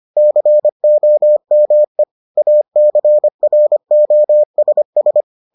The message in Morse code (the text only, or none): COME ACROSS